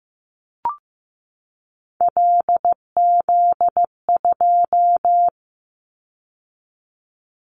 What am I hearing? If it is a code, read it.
Morse code, heard as a keyed tone, about 15 wpm, 700 Hz. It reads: LZ2
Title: Callsign Prefixes for Top 25 DXCC - 15wpm